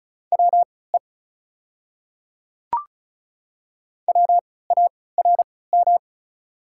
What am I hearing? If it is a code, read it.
Morse code, heard as a keyed tone, about 36 wpm, spaced as for 20 wpm, 690 Hz. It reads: WE WARM
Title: 3x Character Spacing - Top 500 Words - 35wpm